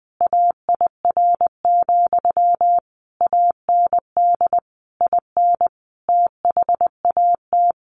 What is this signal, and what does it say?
Morse code, heard as a keyed tone, about 20 wpm, 700 Hz. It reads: AIR, AND IN THAT
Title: A Sound Of Thunder 20wpm